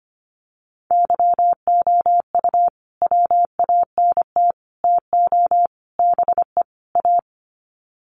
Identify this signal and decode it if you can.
Morse code, heard as a keyed tone, about 25 wpm, 700 Hz. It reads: YOU WANT TO BE A
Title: Sets of 5 Words Encore2 25wpm